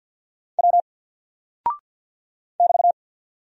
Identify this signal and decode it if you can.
Morse code, heard as a keyed tone, about 50 wpm, 700 Hz. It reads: W <BT>
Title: Single Character (Letter:Number:Punctuation:Prosign) 50wpm